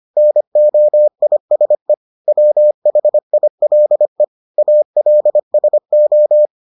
Morse code, heard as a keyed tone, about 25 wpm, 600 Hz. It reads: NOISE WHILE ALSO